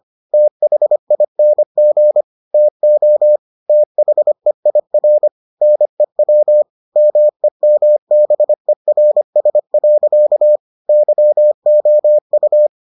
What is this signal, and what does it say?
Morse code, heard as a keyed tone, about 25 wpm, 600 Hz. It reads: THING TO THEIR NEW MEMBERS. YOU